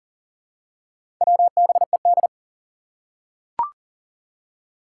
Morse code, heard as a keyed tone, about 40 wpm, 700 Hz. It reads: WBED
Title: Four Challenging Letter-Number Combinations 40wpm